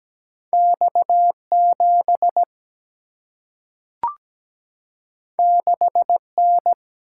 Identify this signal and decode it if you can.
Morse code, heard as a keyed tone, about 17 wpm, 700 Hz. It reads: X7 6N
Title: Single-Letter Single-Number 17wpm